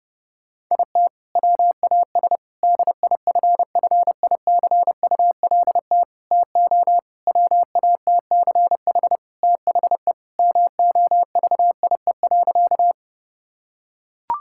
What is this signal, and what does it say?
Morse code, heard as a keyed tone, about 32 wpm, 700 Hz. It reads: IT WAS DIFFICULT TO WATCH THE MOVIE.
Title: Sentences from Top 1000 Words - Review Part 5 - 30wpm